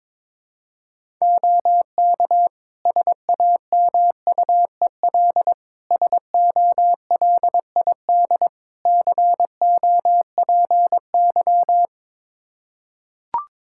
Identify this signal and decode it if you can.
Morse code, heard as a keyed tone, about 21 wpm, 700 Hz. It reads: OK SAMUEL SOLID COPY
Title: K1USN SST QSO Practice 22wpm